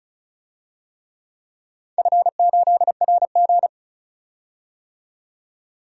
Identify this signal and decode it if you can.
Morse code, heard as a keyed tone, about 36 wpm, 700 Hz. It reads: F8RG
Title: DXCC-100 Call Signs - 4 Characters - 35wpm